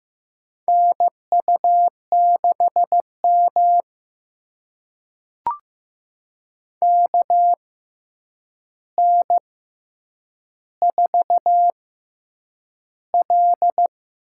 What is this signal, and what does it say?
Morse code, heard as a keyed tone, about 15 wpm, spaced as for 8 wpm, 700 Hz. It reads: NU6M KN4L
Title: USA Call Signs - 4 Characters - 6x Character Spacing - 15wpm